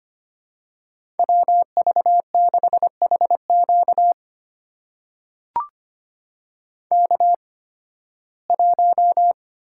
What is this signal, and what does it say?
Morse code, heard as a keyed tone, about 25 wpm, spaced as for 16 wpm, 700 Hz. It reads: WV6HQ K1
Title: USA Call Signs - 5 Characters - 8x Character Spacing - 25wpm